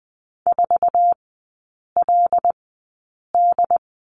Morse code, heard as a keyed tone, about 20 wpm, spaced as for 8 wpm, 700 Hz. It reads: 4LD
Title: 074 Lesson 13e - Letter L - Three Characters 20wpm